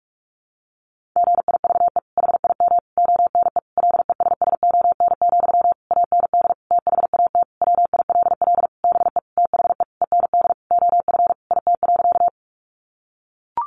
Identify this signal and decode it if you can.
Morse code, heard as a keyed tone, about 44 wpm, 700 Hz. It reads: GIVE HIM ONE LESSON, AND THAT WILL BE THE END OF IT.
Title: Sentences from Top 1200 Words 45wpm